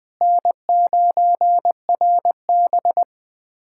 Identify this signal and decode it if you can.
Morse code, heard as a keyed tone, about 20 wpm, 700 Hz. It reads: N9RB
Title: USA Callsign DE Callsign - 4 Characters - 20wpm